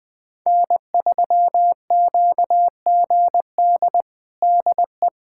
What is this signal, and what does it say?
Morse code, heard as a keyed tone, about 20 wpm, 700 Hz. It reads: N3QGD DE
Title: USA Callsign DE Callsign - 5 Characters - 20wpm